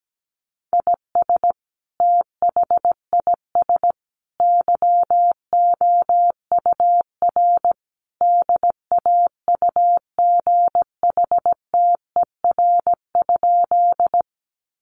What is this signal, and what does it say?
Morse code, heard as a keyed tone, about 17 wpm, 700 Hz. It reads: IS THIS YOUR DAUGHTER?
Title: Sentences from Top 1500 Words 17wpm